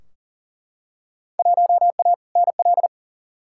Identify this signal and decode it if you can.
Morse code, heard as a keyed tone, about 40 wpm, 700 Hz. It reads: 1A NL
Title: Field Day Exchanges Encore - 40wpm